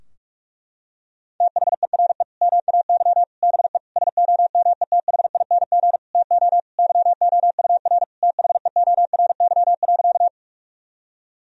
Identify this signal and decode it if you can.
Morse code, heard as a keyed tone, about 46 wpm, 700 Hz. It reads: THERE MAY BE SOMETHING TO YOUR THEORY.
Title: Sentences from Top 1000 Words - Review Part 3 - 45wpm